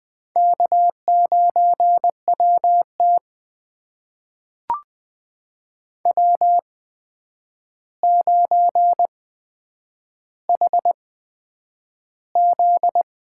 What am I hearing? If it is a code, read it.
Morse code, heard as a keyed tone, about 20 wpm, spaced as for 9 wpm, 700 Hz. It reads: K9WT W9HZ
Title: USA Call Signs - 4 Characters - 8x Character Spacing - 20wpm